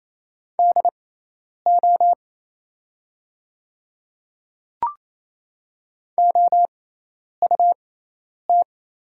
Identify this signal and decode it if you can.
Morse code, heard as a keyed tone, about 27 wpm, spaced as for 10 wpm, 690 Hz. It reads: DO OUT
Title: 6x Character Spacing - Top 100 Words - 28wpm